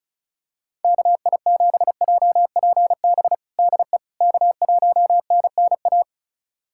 Morse code, heard as a keyed tone, about 36 wpm, 700 Hz. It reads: KI7JPB DE K1NNA
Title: USA Callsign DE Callsign 35wpm